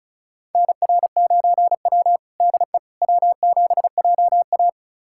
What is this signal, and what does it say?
Morse code, heard as a keyed tone, about 34 wpm, 700 Hz. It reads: NR9W DE W7JA
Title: USA Callsign DE Callsign - 4 Characters - 35wpm